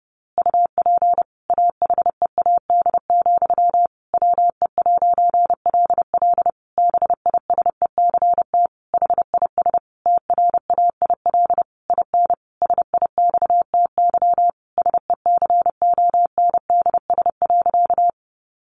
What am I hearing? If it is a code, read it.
Morse code, heard as a keyed tone, about 30 wpm, 700 Hz. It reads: UP AHEAD, WE'LL BISECT HIS TRAIL IN SIXTY SECONDS.
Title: A Sound Of Thunder 30wpm